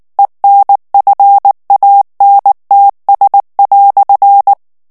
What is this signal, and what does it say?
Morse code, heard as a keyed tone, about 19 wpm, 800 Hz. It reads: ENFANTS"
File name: morse mp3 / FRANCAIS TEXTE CLAIR 1 VITESSE 18 MOTS MINUTES